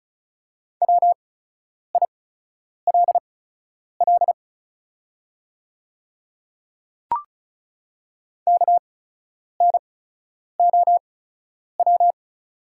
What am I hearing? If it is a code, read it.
Morse code, heard as a keyed tone, about 36 wpm, spaced as for 10 wpm, 700 Hz. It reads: WILL KNOW
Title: 8x Character Spacing - Top 100 Words - 35wpm